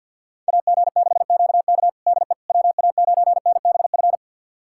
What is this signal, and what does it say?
Morse code, heard as a keyed tone, about 52 wpm, 700 Hz. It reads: AG6QK DE WA9NBF
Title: USA Callsign DE Callsign 50wpm